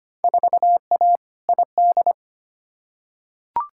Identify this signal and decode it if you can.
Morse code, heard as a keyed tone, about 25 wpm, 700 Hz. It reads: 4A ID
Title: Field Day Exchanges Encore - 25wpm